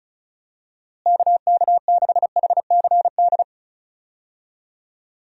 Morse code, heard as a keyed tone, about 34 wpm, 700 Hz. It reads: KK6HCD
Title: USA Call Signs - 6 Characters - Repeated 3x - 35wpm